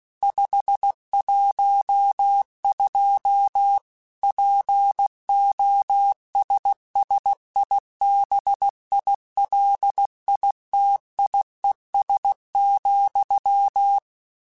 Morse code, heard as a keyed tone, about 16 wpm, 800 Hz. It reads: 512 POSSIBILITIES,
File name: Technical Subjects / Technical Subjects-mp3s / AND Gate @15